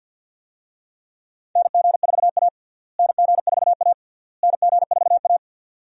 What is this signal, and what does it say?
Morse code, heard as a keyed tone, about 52 wpm, 700 Hz. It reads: NG4A NG4A NG4A
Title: USA Call Signs - 4 Characters - Repeated 3x - 50wpm